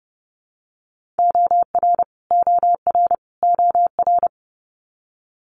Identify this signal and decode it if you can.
Morse code, heard as a keyed tone, about 32 wpm, 700 Hz. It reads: OR OR OR
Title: Multi-letter Phonemes Repeated 3x 30wpm